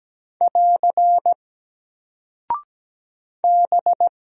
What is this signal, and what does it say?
Morse code, heard as a keyed tone, about 17 wpm, 700 Hz. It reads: <AR> B
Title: Single Character (Letter:Number:Punctuation:Prosign) 17wpm